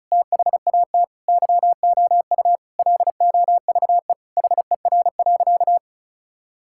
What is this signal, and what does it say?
Morse code, heard as a keyed tone, about 36 wpm, 700 Hz. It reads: THAT YOU LOVE HER.
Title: Sentences from Top 1600 Words 35wpm